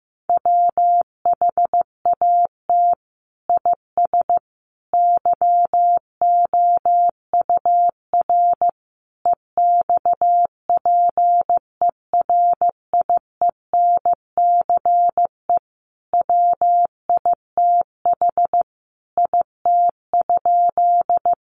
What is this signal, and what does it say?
Morse code, heard as a keyed tone, about 15 wpm, 700 Hz. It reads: WHAT IS YOUR EXPERIENCE WITH IT?
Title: Sentences from Top 600 Words 15wpm